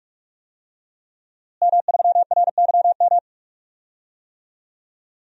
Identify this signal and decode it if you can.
Morse code, heard as a keyed tone, about 46 wpm, 700 Hz. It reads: M3RYM
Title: DXCC-100 Call Signs - 5 Characters - Repeated 2x - 45wpm